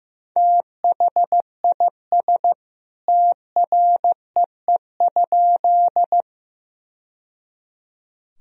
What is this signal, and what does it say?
Morse code, heard as a keyed tone, about 15 wpm, 700 Hz. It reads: THIS TREE?
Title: Sentences from Top 1000 Words - Review Part 1 - 15wpm